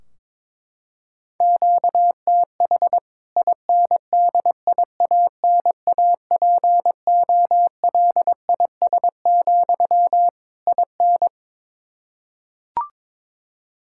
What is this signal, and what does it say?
Morse code, heard as a keyed tone, about 22 wpm, 700 Hz. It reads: QTH INDIANAPOLIS, IN
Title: K1USN SST QSO Practice 22wpm